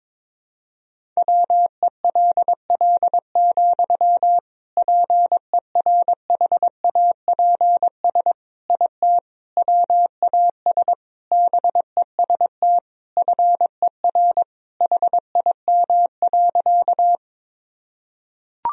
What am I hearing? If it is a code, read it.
Morse code, heard as a keyed tone, about 22 wpm, 700 Hz. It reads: WELL, PERHAPS IT WAS BEST FER HIM.
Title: Sentences from Top 500 Words and Abbreviations 22wpm